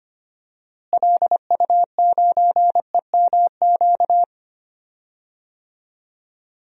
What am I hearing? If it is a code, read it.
Morse code, heard as a keyed tone, about 25 wpm, 700 Hz. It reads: LU9EMQ
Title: DXCC-100 Call Signs - 6 Characters - Repeated 2x - 25wpm